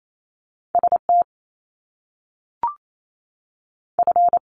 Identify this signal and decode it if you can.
Morse code, heard as a keyed tone, about 28 wpm, spaced as for 3 wpm, 700 Hz. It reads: S TF
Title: Top Two Letter Combinations in Words 28wpm